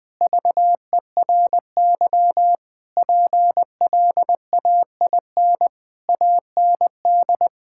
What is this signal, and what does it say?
Morse code, heard as a keyed tone, about 20 wpm, 700 Hz. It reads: VERY PLAIN AND
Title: Sentences from Top 1000 Words - Review Part 5 - 20wpm